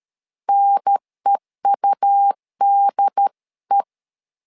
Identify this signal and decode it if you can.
Morse code, heard as a keyed tone, about 13 wpm, 800 Hz. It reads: NEUDE
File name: LEZIONI / 3 MWRO / 3 M 12wpm 1541